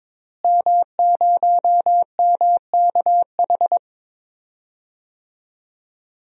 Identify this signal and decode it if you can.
Morse code, heard as a keyed tone, about 22 wpm, 700 Hz. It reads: M0MKH
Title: DXCC-100 Call Signs - 5 Characters - Repeated 2x - 22wpm